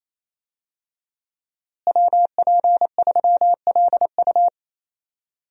Morse code, heard as a keyed tone, about 27 wpm, 700 Hz. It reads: WP3LU
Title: USA Call Signs - 5 Characters - Repeated 2x - 28wpm